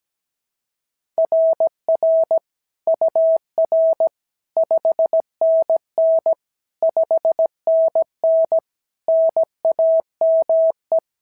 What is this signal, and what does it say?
Morse code, heard as a keyed tone, about 17 wpm, 690 Hz. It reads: RR UR 5NN 5NN NAME
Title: SOTA QSO Practice 17wpm